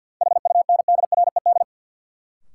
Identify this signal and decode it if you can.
Morse code, heard as a keyed tone, about 53 wpm, 700 Hz. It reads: HUNDRED
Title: Sentences from Top 400 Words - New Word Drill - 50wpm